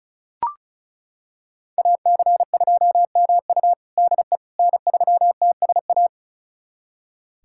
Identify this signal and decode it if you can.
Morse code, heard as a keyed tone, about 34 wpm, 700 Hz. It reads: AC2MU DE N3TSA
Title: USA Callsign DE Callsign - 5 Characters - 35wpm